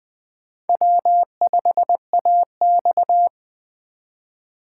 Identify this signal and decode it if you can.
Morse code, heard as a keyed tone, about 19 wpm, 700 Hz. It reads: W5AX